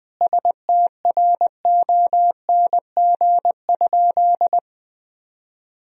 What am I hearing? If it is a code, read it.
Morse code, heard as a keyed tone, about 20 wpm, 700 Hz. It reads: STRONG?
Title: Sentences from Top 1000 Words - Review Part 3 - 20wpm